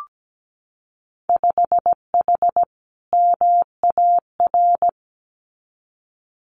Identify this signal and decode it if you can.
Morse code, heard as a keyed tone, about 17 wpm, 700 Hz. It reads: 5H MAR
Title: Winter Field Day Exchanges Encore 17wpm